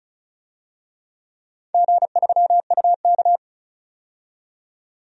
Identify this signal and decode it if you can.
Morse code, heard as a keyed tone, about 36 wpm, 700 Hz. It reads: G3UK
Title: DXCC-100 Call Signs - 4 Characters - 35wpm